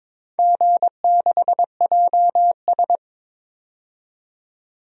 Morse code, heard as a keyed tone, about 22 wpm, 700 Hz. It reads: G6JS
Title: DXCC-100 Call Signs - 4 Characters - 22wpm